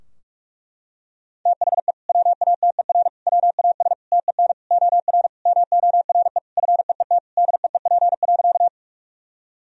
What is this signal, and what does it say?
Morse code, heard as a keyed tone, about 46 wpm, 700 Hz. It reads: THE WATER WAS TEN OR MORE FEET DEEP.
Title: Sentences from Top 1000 Words - Review Part 3 - 45wpm